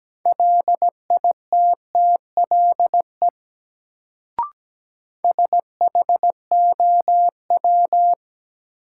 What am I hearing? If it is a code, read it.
Morse code, heard as a keyed tone, about 17 wpm, 700 Hz. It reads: LITTLE SHOW
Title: Sentences from Top 200 Words - New Word Drill - 17wpm